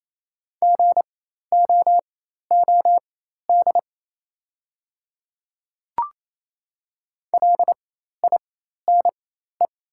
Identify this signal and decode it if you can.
Morse code, heard as a keyed tone, about 28 wpm, spaced as for 13 wpm, 700 Hz. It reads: GOOD LINE
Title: 4x Character Spacing - Top 200 Words - 28wpm